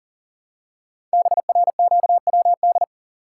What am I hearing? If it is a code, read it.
Morse code, heard as a keyed tone, about 42 wpm, 700 Hz. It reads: BRQWD
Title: Code Groups - 5 Letters - 40wpm